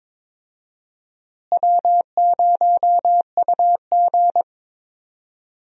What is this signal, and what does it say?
Morse code, heard as a keyed tone, about 21 wpm, 700 Hz. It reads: W0UG
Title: USA Call Signs - 4 Characters - Repeated 3x - 22wpm